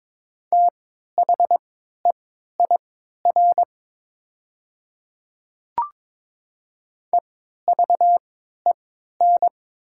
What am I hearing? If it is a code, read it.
Morse code, heard as a keyed tone, about 22 wpm, spaced as for 12 wpm, 690 Hz. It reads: THEIR EVEN
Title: 3x Character Spacing - Top 200 Words - 22wpm